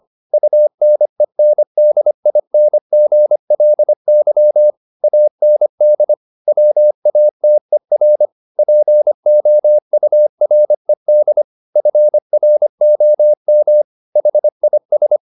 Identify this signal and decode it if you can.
Morse code, heard as a keyed tone, about 25 wpm, 600 Hz. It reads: UNENDINGLY AND WATER POURED FROM HIS
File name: the old man and the sea part 1 25wpm